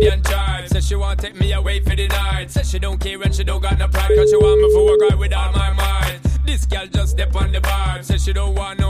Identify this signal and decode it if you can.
Ringback tone (Europe, 425 Hz).